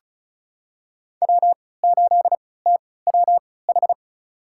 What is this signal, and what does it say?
Morse code, heard as a keyed tone, about 36 wpm, spaced as for 20 wpm, 700 Hz. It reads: W8TWH
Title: USA Call Signs - 5 Characters - 3x Character Spacing - 35wpm